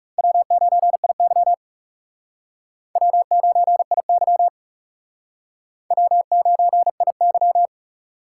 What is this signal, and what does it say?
Morse code, heard as a keyed tone, about 37 wpm, 700 Hz. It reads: W9IY W9IY W9IY
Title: Speed Racing - USA 1x2 Call Signs 30wpm